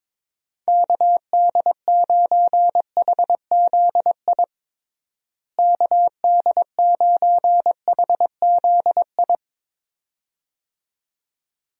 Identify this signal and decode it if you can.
Morse code, heard as a keyed tone, about 22 wpm, 700 Hz. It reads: KD9HZI KD9HZI
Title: USA Call Signs - 6 Characters - Repeated 2x - 22wpm